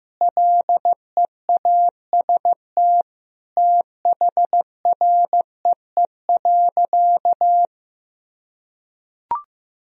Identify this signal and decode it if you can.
Morse code, heard as a keyed tone, about 15 wpm, 700 Hz. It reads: LEAST THREE.
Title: Sentences from Top 1000 Words - Review Part 1 - 15wpm